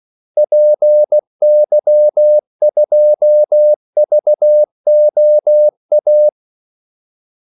Morse code, heard as a keyed tone, about 16 wpm, 600 Hz. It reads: PY2VOA
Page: Kaldesignaler 16 wpm